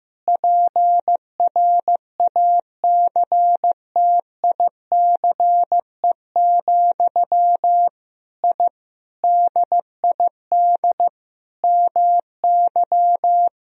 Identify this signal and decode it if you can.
Morse code, heard as a keyed tone, about 15 wpm, 700 Hz. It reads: PRACTICE, I DID MY
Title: Sentences from Top 1000 Words - Review Part 4 - 15wpm